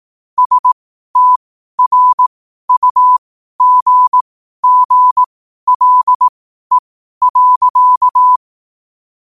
Morse code, heard as a keyed tone, about 18 wpm, spaced as for 12 wpm, 1000 Hz. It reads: STRUGGLE.
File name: qotdfiles / 12wpm